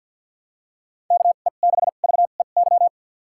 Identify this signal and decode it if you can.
Morse code, heard as a keyed tone, about 50 wpm, spaced as for 36 wpm, 700 Hz. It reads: KE6VEY
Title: USA Call Signs - 2x Character Spacing - 50wpm